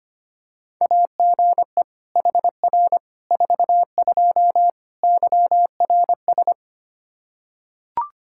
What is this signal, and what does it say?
Morse code, heard as a keyed tone, about 25 wpm, 700 Hz. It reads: AGE HR 42 YRS
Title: Practice QSO - Exchange 3 - More Info - 25wpm